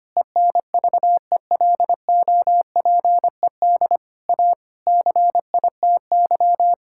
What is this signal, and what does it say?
Morse code, heard as a keyed tone, about 25 wpm, 700 Hz. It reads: ENVELOPED A CITY
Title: Pangram Challenge v2 - 25wpm